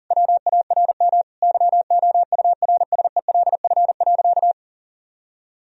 Morse code, heard as a keyed tone, about 40 wpm, 700 Hz. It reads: WARM YOURSELF.
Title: Sentences from Top 1000 Words - Review Part 2 - 40wpm